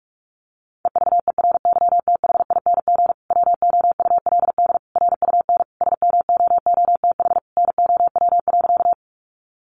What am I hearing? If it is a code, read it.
Morse code, heard as a keyed tone, about 44 wpm, 700 Hz. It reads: EVERYTHING WOULD RUN SMOOTH NOW.
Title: Sentences from Top 1400 Words 45wpm